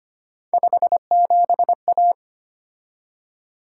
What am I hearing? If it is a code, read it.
Morse code, heard as a keyed tone, about 25 wpm, 700 Hz. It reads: 57A
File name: Three Letter-Number Combinations - 25wpm